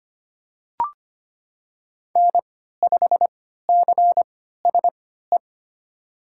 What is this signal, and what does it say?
Morse code, heard as a keyed tone, about 25 wpm, spaced as for 14 wpm, 700 Hz. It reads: N5CSE
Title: USA Call Signs - 5 Characters - 3x Character Spacing - 25wpm